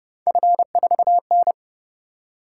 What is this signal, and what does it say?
Morse code, heard as a keyed tone, about 28 wpm, 700 Hz. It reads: F4N